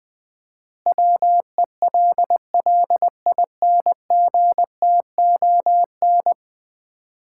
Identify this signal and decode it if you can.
Morse code, heard as a keyed tone, about 20 wpm, 700 Hz. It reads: WELLINGTON
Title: Common QSO Phrases - 4x Word-Spacing - 20wpm